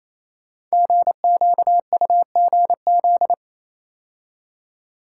Morse code, heard as a keyed tone, about 29 wpm, 700 Hz. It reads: GQUGZ
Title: Code Groups - 5 Letters - 28wpm